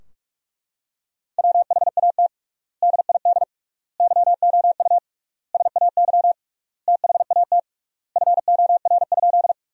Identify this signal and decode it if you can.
Morse code, heard as a keyed tone, about 41 wpm, 700 Hz. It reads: WHAT DID YOU SAY THAT FOR?
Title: 3x Word-Spacing - Sentences from Top 200 Words - 45wpm